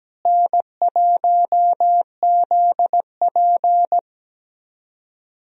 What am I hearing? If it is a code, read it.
Morse code, heard as a keyed tone, about 17 wpm, 700 Hz. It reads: N1ZP
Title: USA Call Signs - 4 Characters - Repeated 3x - 17wpm